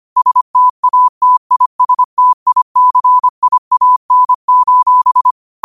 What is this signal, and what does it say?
Morse code, heard as a keyed tone, about 25 wpm, 1000 Hz. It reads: STATISTICIAN:
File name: qotdfiles / 25wpm